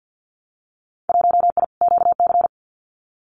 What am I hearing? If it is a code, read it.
Morse code, heard as a keyed tone, about 48 wpm, 700 Hz. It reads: 1I QC
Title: Winter Field Day Exchanges 50wpm